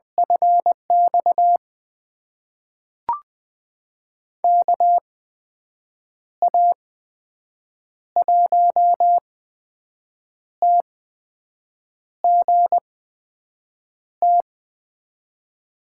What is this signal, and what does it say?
Morse code, heard as a keyed tone, about 21 wpm, spaced as for 6 wpm, 700 Hz. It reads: FX KA1TGT
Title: USA Call Signs - 6 Characters - 8x Character Spacing - 20wpm